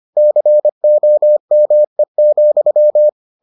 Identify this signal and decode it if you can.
Morse code, heard as a keyed tone, about 25 wpm, 600 Hz. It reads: COME,